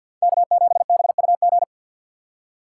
Morse code, heard as a keyed tone, about 61 wpm, 700 Hz. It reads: K7BUG
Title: Speed Racing - USA 1x3 Call Signs 50wpm